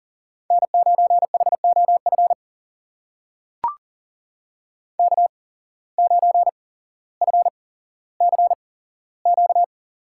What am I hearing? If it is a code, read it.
Morse code, heard as a keyed tone, about 40 wpm, spaced as for 17 wpm, 700 Hz. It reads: N9HOF K9FCQ
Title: USA Call Signs - 5 Characters - 8x Character Spacing - 40wpm